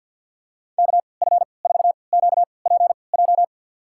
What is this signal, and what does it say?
Morse code, heard as a keyed tone, about 51 wpm, spaced as for 28 wpm, 700 Hz. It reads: KF4QPJ
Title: USA Call Signs - 6 Characters - 3x Character Spacing - 50wpm